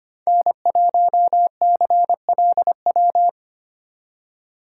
Morse code, heard as a keyed tone, about 31 wpm, 700 Hz. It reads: N1CLW
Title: Speed Racing - USA Call Signs 25wpm